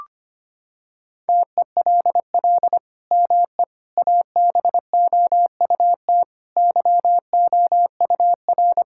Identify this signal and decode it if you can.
Morse code, heard as a keyed tone, about 25 wpm, 700 Hz. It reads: TELL ME ABOUT YOUR
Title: Sentences from Top 1600 Words 25wpm